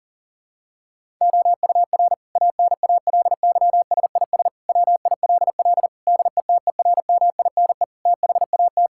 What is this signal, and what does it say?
Morse code, heard as a keyed tone, about 37 wpm, 700 Hz. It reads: OUR ANALYSIS WILL DETERMINE THAT
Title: Sentences from Top 1000 Words - Review Part 5 - 40wpm